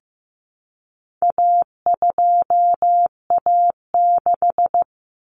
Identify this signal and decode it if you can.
Morse code, heard as a keyed tone, about 15 wpm, 700 Hz. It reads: A2A6
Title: Common Cut Numbers - 4-Digit 15wpm